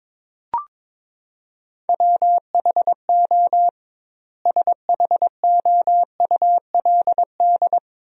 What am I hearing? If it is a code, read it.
Morse code, heard as a keyed tone, about 22 wpm, 700 Hz. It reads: WHO SHOULD
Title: 2x Word-Spacing - Sentences from Top 300 Words - 22wpm